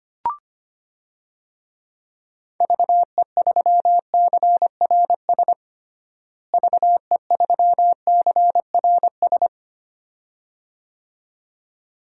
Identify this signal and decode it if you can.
Morse code, heard as a keyed tone, about 25 wpm, 700 Hz. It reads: VE3CRS VE3CRS
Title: DXCC-100 Call Signs - 6 Characters - Repeated 2x - 25wpm